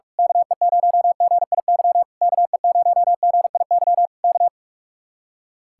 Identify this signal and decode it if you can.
Morse code, heard as a keyed tone, about 44 wpm, 700 Hz. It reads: KE0GIY KE0GIY K
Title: Speed-Racing Common QSO Phrases Encore - 45wpm